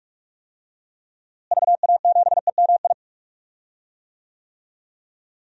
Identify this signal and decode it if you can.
Morse code, heard as a keyed tone, about 45 wpm, 700 Hz. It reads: VA7EMI
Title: DXCC-100 Call Signs - 6 Characters - Repeated 2x - 45wpm